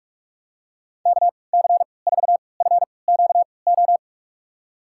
Morse code, heard as a keyed tone, about 46 wpm, spaced as for 26 wpm, 690 Hz. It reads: KC4FQO
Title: USA Call Signs - 6 Characters - 3x Character Spacing - 45wpm